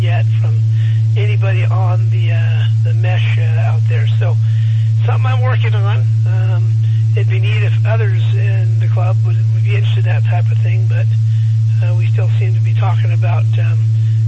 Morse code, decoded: none

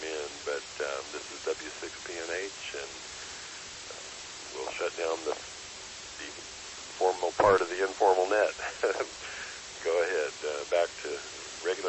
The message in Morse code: none

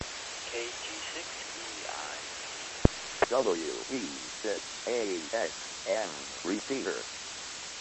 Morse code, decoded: none

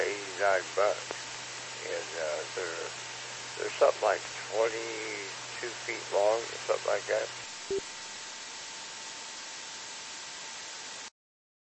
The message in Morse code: none

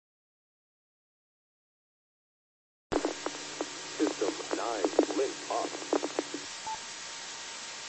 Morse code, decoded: none

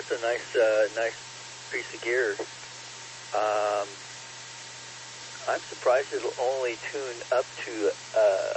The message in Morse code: none

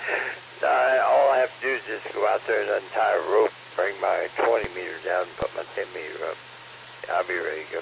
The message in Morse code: none

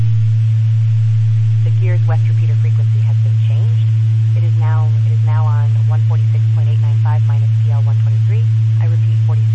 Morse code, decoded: none

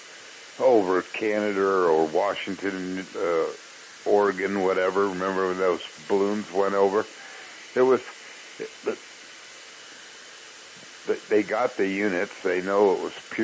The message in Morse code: none

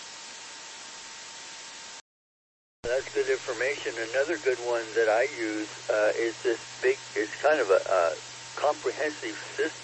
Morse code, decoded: none

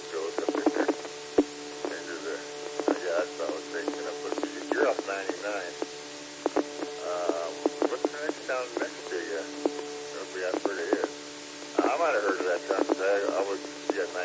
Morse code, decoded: none

